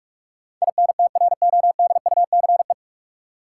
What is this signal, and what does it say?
Morse code, heard as a keyed tone, about 46 wpm, 700 Hz. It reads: INTRODUCE
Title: Sentences from Top 1600 Words - New Word Drill - 45wpm